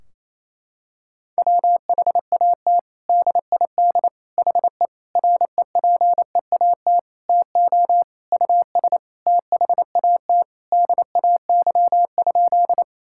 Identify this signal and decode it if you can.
Morse code, heard as a keyed tone, about 28 wpm, 700 Hz. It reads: WHAT DID HE REPEAT TO US THAT DAY?
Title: Sentences from Top 1600 Words 28wpm